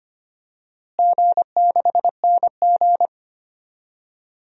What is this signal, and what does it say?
Morse code, heard as a keyed tone, about 25 wpm, 700 Hz. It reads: G6NG
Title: DXCC-100 Call Signs - 4 Characters - Repeated 2x - 25wpm